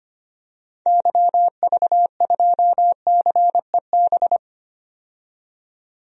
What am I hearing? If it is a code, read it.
Morse code, heard as a keyed tone, about 25 wpm, 690 Hz. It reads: YV2CEB
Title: DXCC-100 Call Signs - 6 Characters - 25wpm